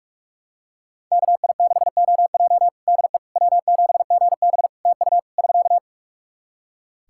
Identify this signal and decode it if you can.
Morse code, heard as a keyed tone, about 46 wpm, 700 Hz. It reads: KI6OJ DE W7GB TU <SK>
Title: Practice QSO - Exchange 4 - Ending - 45wpm